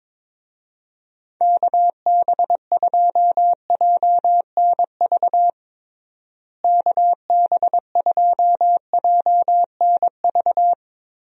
Morse code, heard as a keyed tone, about 22 wpm, 700 Hz. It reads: KB2JNV KB2JNV
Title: USA Call Signs - 6 Characters - Repeated 3x - 22wpm